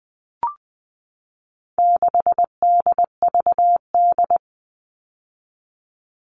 Morse code, heard as a keyed tone, about 20 wpm, 700 Hz. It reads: 6DVD
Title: All Cut Numbers - 4-Digit 20wpm